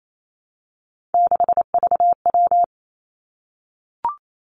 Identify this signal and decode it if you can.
Morse code, heard as a keyed tone, about 28 wpm, 700 Hz. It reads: 6VW
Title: Three Challenging Letter-Number Combinations 28wpm